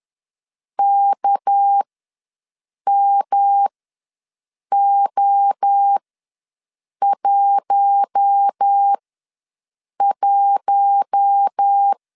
Morse code, heard as a keyed tone, about 10 wpm, spaced as for 6 wpm, 800 Hz. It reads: KMO11
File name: LEZIONI / 6 PKX0 / 6 PK ws 10wpm 3745